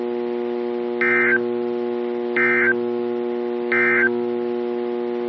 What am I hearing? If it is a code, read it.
Emergency Alert System (SAME) tones, heard as harsh buzzing bursts. It is heard from 1.0 to 4.1 s.
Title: NOAA Weather Radio - KEC55